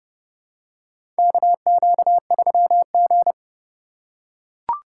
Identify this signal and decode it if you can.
Morse code, heard as a keyed tone, about 30 wpm, 700 Hz. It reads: KQ3G
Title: USA Call Signs - 4x Character Spacing - 30wpm